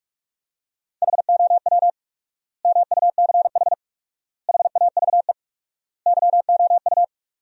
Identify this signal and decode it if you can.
Morse code, heard as a keyed tone, about 46 wpm, 700 Hz. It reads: HOW MUCH HAVE YOU
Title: 4x Word-Spacing - Sentences from Top 300 Words - 45wpm